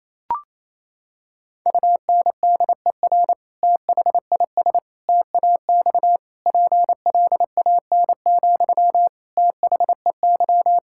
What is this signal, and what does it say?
Morse code, heard as a keyed tone, about 28 wpm, 700 Hz. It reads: UNDER THIS TAX PLAN, THEY
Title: Sentences from Top 1000 Words - Review Part 5 - 28wpm